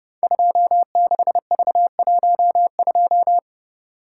Morse code, heard as a keyed tone, about 30 wpm, 700 Hz. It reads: 26V12
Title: Code Groups - 5 Letter:Numbers - 30wpm